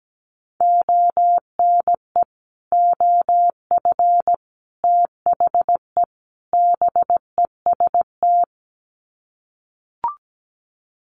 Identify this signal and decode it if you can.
Morse code, heard as a keyed tone, about 17 wpm, 700 Hz. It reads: ONE OF THE BEST
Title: Sets of 4 Words 17wpm